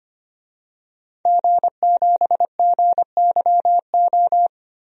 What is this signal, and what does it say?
Morse code, heard as a keyed tone, about 25 wpm, 700 Hz. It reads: G7GYO
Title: DXCC-100 Call Signs - 5 Characters - Repeated 2x - 25wpm